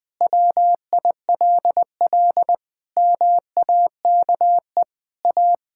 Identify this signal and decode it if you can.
Morse code, heard as a keyed tone, about 20 wpm, 700 Hz. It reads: WILL MAKE A
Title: Pangram Challenge v2 - 20wpm